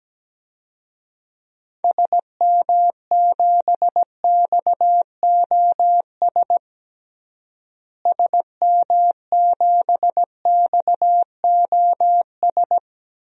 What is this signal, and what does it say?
Morse code, heard as a keyed tone, about 17 wpm, 700 Hz. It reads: SM7XOS SM7XOS
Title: DXCC-100 Call Signs - 6 Characters - Repeated 2x - 17wpm